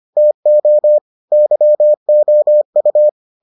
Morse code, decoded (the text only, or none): TO YOU